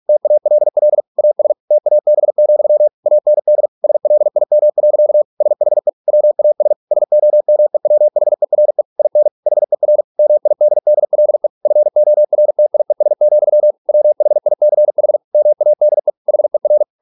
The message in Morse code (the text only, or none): TALL AS TAB, AND SLIM. SHE WAS SOMEWHERE IN HER MIDDLE FORTIES, WHICH MADE HER